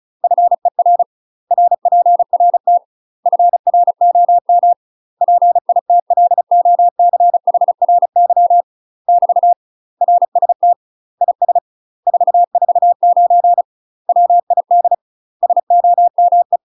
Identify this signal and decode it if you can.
Morse code, heard as a keyed tone, about 33 wpm, 700 Hz. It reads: FER RPRT FROM PITLOCHRY <BT> RST IS 449 WID SOME
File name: UK MASTER QSOs TRAINING / 35WPM / UK MASTER QSOs 0028